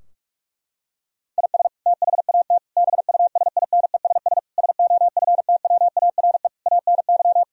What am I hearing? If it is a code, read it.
Morse code, heard as a keyed tone, about 41 wpm, 700 Hz. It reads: IS THAT BUSINESS SOFTWARE ANY
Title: Sentences from Top 1000 Words - Review Part 3 - 45wpm